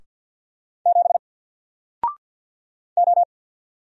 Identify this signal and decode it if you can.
Morse code, heard as a keyed tone, about 50 wpm, 700 Hz. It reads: 7 O
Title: Single Letter-Number Alt 50wpm